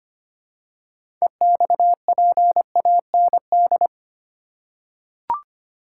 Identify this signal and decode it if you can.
Morse code, heard as a keyed tone, about 25 wpm, 700 Hz. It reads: EXPAND
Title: Six Letter Words - 25wpm